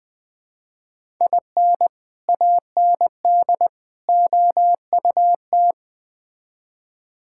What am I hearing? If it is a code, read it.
Morse code, heard as a keyed tone, about 21 wpm, 700 Hz. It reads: IN AND OUT
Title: Build A Phrase - 4-Word Phrases - 20wpm